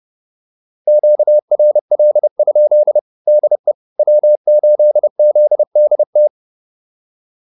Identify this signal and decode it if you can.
Morse code, heard as a keyed tone, about 30 wpm, 600 Hz. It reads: QRL? DE W8ZDT
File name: QSO108 30